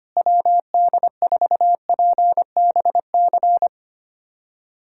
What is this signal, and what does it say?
Morse code, heard as a keyed tone, about 25 wpm, 700 Hz. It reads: WD4PBC
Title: USA Callsign DE Callsign - 6 Characters - 25wpm